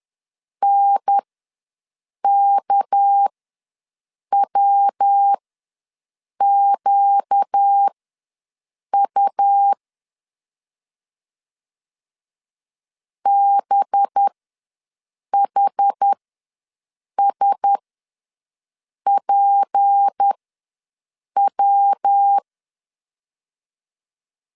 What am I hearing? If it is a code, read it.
Morse code, heard as a keyed tone, about 10 wpm, spaced as for 6 wpm, 800 Hz. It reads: NKWQU BHSPW
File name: LEZIONI / 6 PKX0 / 6 PK ws 10wpm 3745